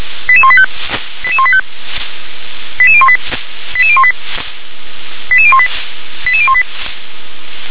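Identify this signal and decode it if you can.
Five-tone selective call (ZVEI), heard as quick runs of five tones, tones of about 70 ms. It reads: 88185 88185 88018 88018 88018 88018
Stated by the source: ZVEI five-tone call received on an SDR.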